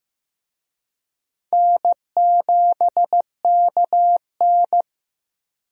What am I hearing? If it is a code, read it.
Morse code, heard as a keyed tone, about 15 wpm, 700 Hz. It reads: N7KN
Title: USA Call Signs - 4 Characters - 15wpm